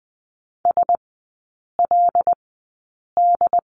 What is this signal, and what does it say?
Morse code, heard as a keyed tone, about 20 wpm, spaced as for 8 wpm, 700 Hz. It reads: SLD
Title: 087 Lesson 15e - Number 5 - Three Characters 20wpm